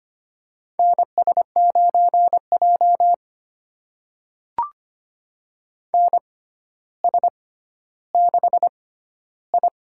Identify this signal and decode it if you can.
Morse code, heard as a keyed tone, about 25 wpm, spaced as for 13 wpm, 700 Hz. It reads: NS9J NS6I
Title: USA Call Signs - 4 Characters - 6x Character Spacing - 25wpm